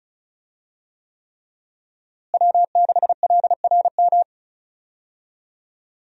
Morse code, heard as a keyed tone, about 36 wpm, 700 Hz. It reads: W6LRM
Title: USA Call Signs - 5 Characters - Repeated 3x - 35wpm